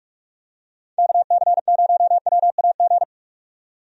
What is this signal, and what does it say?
Morse code, heard as a keyed tone, about 41 wpm, 700 Hz. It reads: KC0WAG
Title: USA Call Signs - 6 Characters - 45wpm